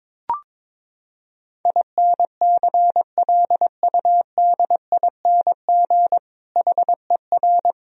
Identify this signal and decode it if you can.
Morse code, heard as a keyed tone, about 22 wpm, 700 Hz. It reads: INCLUDING HER
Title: Sentences from Top 1000 Words - Review Part 4 - 22wpm